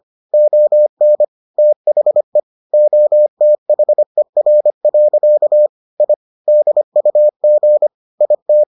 Morse code, heard as a keyed tone, about 25 wpm, 600 Hz. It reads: ON THE OTHER. I DUG IT